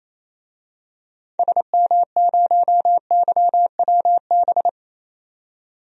Morse code, heard as a keyed tone, about 27 wpm, 700 Hz. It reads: SM0YWB
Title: DXCC-100 Call Signs - 6 Characters - 28wpm